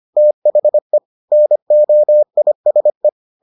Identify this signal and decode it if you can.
Morse code, heard as a keyed tone, about 25 wpm, 600 Hz. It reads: THE NOISE